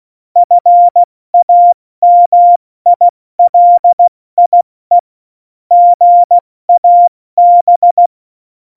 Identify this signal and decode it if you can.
Morse code, heard as a keyed tone, about 16 wpm, 700 Hz. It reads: FAMILIE GAB
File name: Morsekurs / Witze / Witz 19. Februar 2019, 70 F80